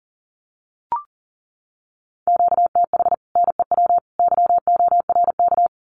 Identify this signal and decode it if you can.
Morse code, heard as a keyed tone, about 39 wpm, 700 Hz. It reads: QTH NEW YORK
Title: Common QSO Phrases 40wpm